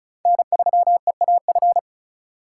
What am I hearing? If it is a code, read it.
Morse code, heard as a keyed tone, about 36 wpm, 700 Hz. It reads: N3EAF
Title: USA Call Signs - 5 Characters - 3x Character Spacing - 35wpm